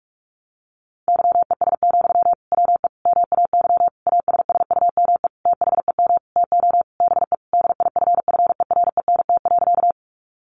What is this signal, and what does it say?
Morse code, heard as a keyed tone, about 44 wpm, 700 Hz. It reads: YES, WE MAY ASSUME THEM TO BE DIFFERENT.
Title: Sentences from Top 1500 Words 45wpm